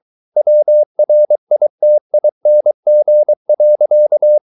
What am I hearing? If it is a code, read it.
Morse code, heard as a keyed tone, about 23 wpm, 600 Hz. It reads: WRITING.